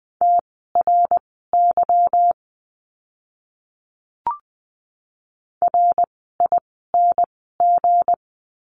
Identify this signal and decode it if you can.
Morse code, heard as a keyed tone, about 20 wpm, spaced as for 14 wpm, 700 Hz. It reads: TRY RING
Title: 160 Lesson 26f - Letter g - Words 20wpm